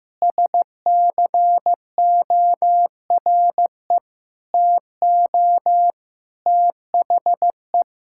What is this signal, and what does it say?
Morse code, heard as a keyed tone, about 15 wpm, 700 Hz. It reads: SCORE TO THE
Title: Sentences from Top 1600 Words 15wpm